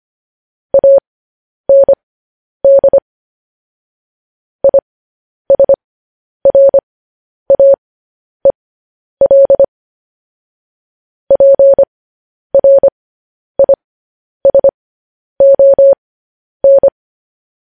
Morse code, heard as a keyed tone, about 25 wpm, spaced as for 10 wpm, 550 Hz. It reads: AND ISRAEL PRISON
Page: news headlines in morse code at 10 wpm